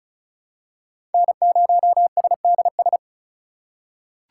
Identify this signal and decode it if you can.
Morse code, heard as a keyed tone, about 36 wpm, 700 Hz. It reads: N0SDS
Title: USA Call Signs - 5 Characters - 4x Character Spacing - 35wpm